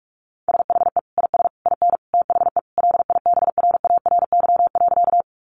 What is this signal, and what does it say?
Morse code, heard as a keyed tone, about 46 wpm, 700 Hz. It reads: SHE IS IN THE LIBRARY.
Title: Sentences from Top 600 Words 45wpm